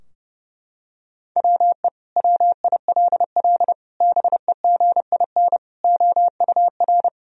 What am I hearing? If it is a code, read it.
Morse code, heard as a keyed tone, about 30 wpm, 700 Hz. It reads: WE WILL BEGIN OUR
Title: Sentences from Top 1000 Words - Review Part 4 - 30wpm